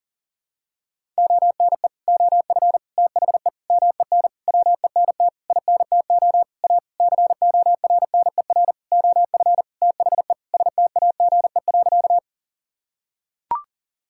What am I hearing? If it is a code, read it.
Morse code, heard as a keyed tone, about 40 wpm, 700 Hz. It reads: ONE OF THE MEN WENT INTO A CORNER OF THE STAGE.
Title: Sentences from Top 1600 Words 40wpm